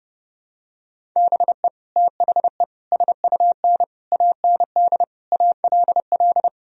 Morse code, heard as a keyed tone, about 32 wpm, 700 Hz. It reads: BE THE SUN AND ALL
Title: Sentences from Top 1000 Words - Review Part 1 - 30wpm